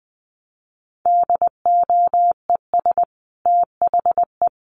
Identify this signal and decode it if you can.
Morse code, heard as a keyed tone, about 20 wpm, 700 Hz. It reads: DOES THE
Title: Sentences from Top 1200 Words 20wpm